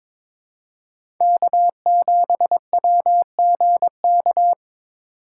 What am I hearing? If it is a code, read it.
Morse code, heard as a keyed tone, about 22 wpm, 700 Hz. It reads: K7WGK
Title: USA Call Signs - 2x Character Spacing - 22wpm